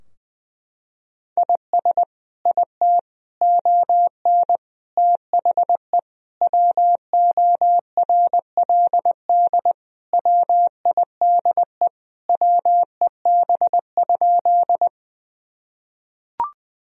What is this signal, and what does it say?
Morse code, heard as a keyed tone, about 20 wpm, 700 Hz. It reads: IS IT ON THE WORLD WIDE WEB?
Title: Sentences from Top 1000 Words - Review Part 4 - 20wpm